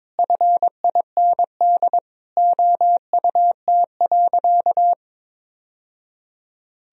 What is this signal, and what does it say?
Morse code, heard as a keyed tone, about 22 wpm, 700 Hz. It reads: FIND OUT.
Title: Sentences from Top 1000 Words - Review Part 1 - 22wpm